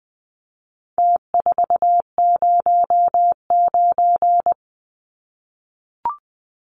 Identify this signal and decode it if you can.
Morse code, heard as a keyed tone, about 20 wpm, 700 Hz. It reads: T409
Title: Numbers - Contest Style - 4-Digit 20wpm